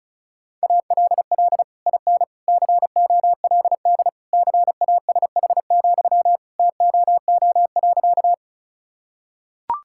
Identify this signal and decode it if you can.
Morse code, heard as a keyed tone, about 36 wpm, 690 Hz. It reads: ALL IN COLD CASH, TOO.
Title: Sentences from Top 1000 Words - Review Part 5 - 35wpm